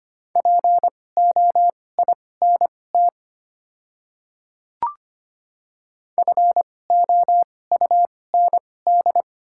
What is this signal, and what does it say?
Morse code, heard as a keyed tone, about 25 wpm, spaced as for 18 wpm, 700 Hz. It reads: POINT FOUND
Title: 2x Character Spacing - Top 500 Words - 25wpm